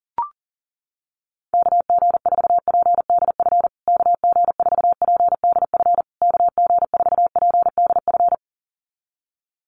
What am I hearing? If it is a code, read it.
Morse code, heard as a keyed tone, about 40 wpm, 700 Hz. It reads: KG4PDF KG4PDF KG4PDF
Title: USA Call Signs Repeated 3x 40wpm